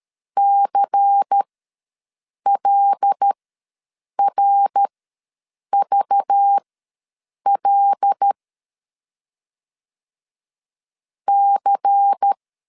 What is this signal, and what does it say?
Morse code, heard as a keyed tone, about 13 wpm, spaced as for 7 wpm, 800 Hz. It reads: CLRVL C
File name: LEZIONI / 5 LBVC / 5 LBVC ws 12wpm 3159